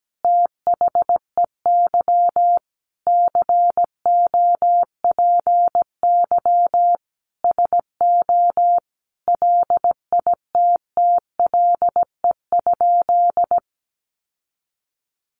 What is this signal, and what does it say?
Morse code, heard as a keyed tone, about 17 wpm, 700 Hz. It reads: THEY COPY SO LITTLE?
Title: Sentences from Top 1400 Words 17wpm